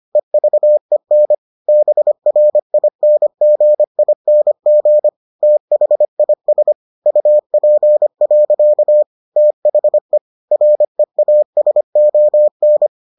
Morse code, heard as a keyed tone, about 25 wpm, 600 Hz. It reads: EVEN BRINGING THIS UP. THE REASON